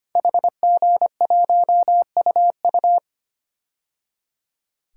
Morse code, heard as a keyed tone, about 25 wpm, 700 Hz. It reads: HG1UU
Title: DXCC-100 Call Signs - 5 Characters - 25wpm